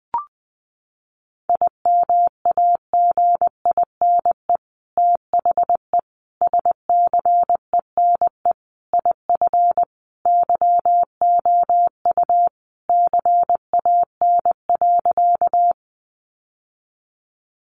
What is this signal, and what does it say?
Morse code, heard as a keyed tone, about 20 wpm, 700 Hz. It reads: IMAGINE THE SCENE IF YOU CAN.
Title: Sentences from Top 1400 Words 20wpm